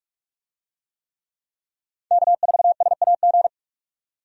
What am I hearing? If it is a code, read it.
Morse code, heard as a keyed tone, about 41 wpm, 700 Hz. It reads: K4SAG
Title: USA Call Signs - 5 Characters - 45wpm